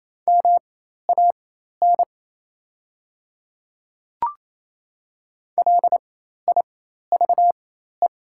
Morse code, heard as a keyed tone, about 27 wpm, spaced as for 13 wpm, 700 Hz. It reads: MAN LIVE
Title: 4x Character Spacing - Top 200 Words - 28wpm